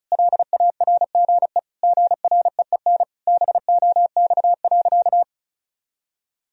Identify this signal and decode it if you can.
Morse code, heard as a keyed tone, about 36 wpm, 700 Hz. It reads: LARGE GREEN BOX.
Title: Sentences from Top 1000 Words - Review Part 2 - 35wpm